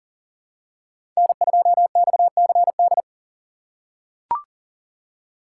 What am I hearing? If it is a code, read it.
Morse code, heard as a keyed tone, about 40 wpm, 700 Hz. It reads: N2XCD
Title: USA Call Signs - 6x Character Spacing - 40wpm